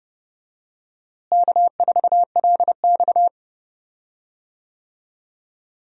Morse code, heard as a keyed tone, about 31 wpm, 700 Hz. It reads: K4LX
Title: USA Call Signs - 4 Characters - Repeated 3x - 30wpm